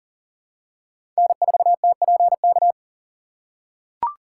Morse code, heard as a keyed tone, about 40 wpm, 700 Hz. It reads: N4TPK
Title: USA Call Signs - 5 Characters - 3x Character Spacing - 40wpm